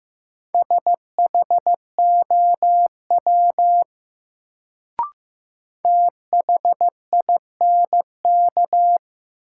Morse code, heard as a keyed tone, about 15 wpm, 700 Hz. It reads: SHOW THINK
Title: Sentences from Top 200 Words - New Word Drill - 15wpm